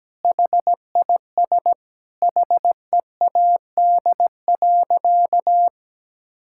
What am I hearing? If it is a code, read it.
Morse code, heard as a keyed tone, about 17 wpm, 700 Hz. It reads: HIS HEAD.
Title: Sentences from Top 1600 Words 17wpm